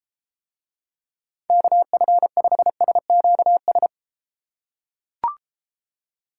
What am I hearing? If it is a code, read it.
Morse code, heard as a keyed tone, about 32 wpm, 700 Hz. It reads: KF5SQS